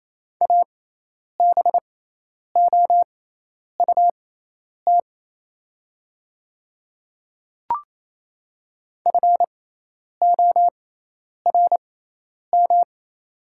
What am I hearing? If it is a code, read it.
Morse code, heard as a keyed tone, about 28 wpm, spaced as for 10 wpm, 700 Hz. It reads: ABOUT FORM
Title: 6x Character Spacing - Top 300 Words - 28wpm